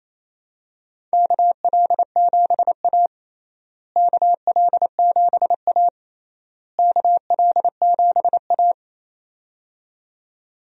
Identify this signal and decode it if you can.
Morse code, heard as a keyed tone, about 28 wpm, 700 Hz. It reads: KL7A KL7A KL7A
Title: DXCC-100 Call Signs - 4 Characters - Repeated 3x - 28wpm